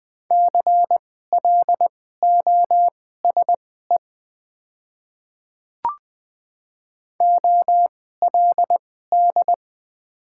Morse code, heard as a keyed tone, about 20 wpm, spaced as for 14 wpm, 700 Hz. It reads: CLOSE OLD
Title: 2x Character Spacing - Top 500 Words - 20wpm